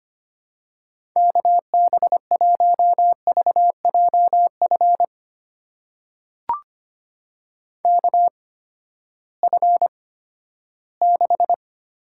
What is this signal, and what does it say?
Morse code, heard as a keyed tone, about 25 wpm, spaced as for 14 wpm, 700 Hz. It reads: KB1VJF KF6
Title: USA Call Signs - 6 Characters - 8x Character Spacing - 25wpm